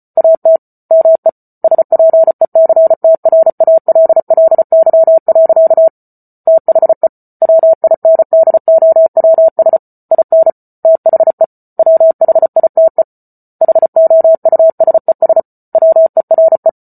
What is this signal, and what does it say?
Morse code, heard as a keyed tone, about 34 wpm, 650 Hz. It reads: AT ME SPECTRALLY. THE WINDOWS IN THE WHITE HOUSES WERE